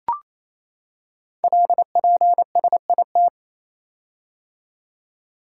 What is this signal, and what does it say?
Morse code, heard as a keyed tone, about 28 wpm, 700 Hz. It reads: LPSIT
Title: Code Groups - 5 Letters - 28wpm